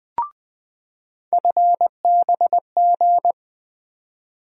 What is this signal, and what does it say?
Morse code, heard as a keyed tone, about 20 wpm, 700 Hz. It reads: FBG
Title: USA Call Sign Postfixes - 3 Letters 20wpm